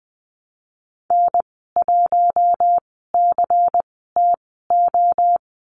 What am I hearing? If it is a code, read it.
Morse code, heard as a keyed tone, about 20 wpm, spaced as for 15 wpm, 700 Hz. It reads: N1CTO
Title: 173 Lesson 28g - Number 9 - Call Signs 20wpm